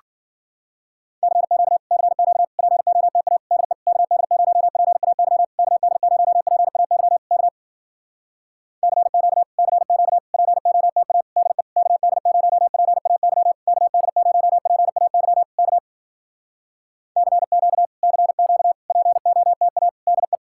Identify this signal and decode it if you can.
Morse code, heard as a keyed tone, about 54 wpm, 700 Hz. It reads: CQ CQ POTA DE KD0PAY KD0PAY K CQ CQ POTA DE KD0PAY KD0PAY K CQ CQ POTA DE